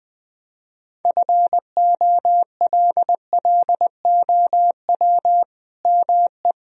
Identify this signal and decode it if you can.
Morse code, heard as a keyed tone, about 20 wpm, 700 Hz. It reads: FOLLOW ME
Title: Sentences from Top 1000 Words - Review Part 1 - 20wpm